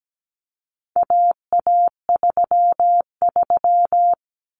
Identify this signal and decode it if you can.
Morse code, heard as a keyed tone, about 17 wpm, 700 Hz. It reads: AA33
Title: Common Cut Numbers - 4-Digit 17wpm